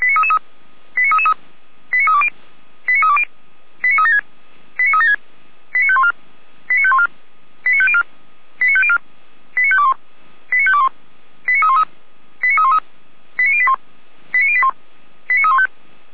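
Five-tone selective call (ZVEI), heard as quick runs of five tones, tones of about 70 ms. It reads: 89333 89320 89320 89376 89376 89514 89514 89504 89504 89521 89521 89313 89313 89081 89081 89315
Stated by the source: ZVEI five-tone call received by SDR, archived by Feuerwehr Orlamünde (fire brigade).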